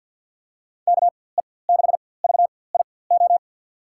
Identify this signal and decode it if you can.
Morse code, heard as a keyed tone, about 52 wpm, spaced as for 24 wpm, 700 Hz. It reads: KE6VIO
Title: USA Call Signs - 6 Characters - 4x Character Spacing - 50wpm